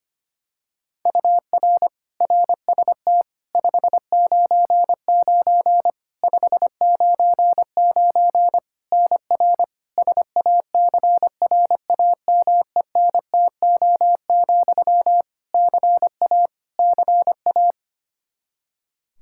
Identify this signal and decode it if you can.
Morse code, heard as a keyed tone, about 25 wpm, 700 Hz. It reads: UR RST 599 599 NR SACRAMENTO, CA CA
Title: K1USN SST QSO Practice 25wpm